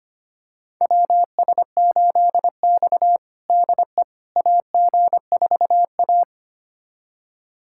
Morse code, heard as a keyed tone, about 25 wpm, 700 Hz. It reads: WS8X DE AG4A
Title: USA Callsign DE Callsign - 4 Characters - 25wpm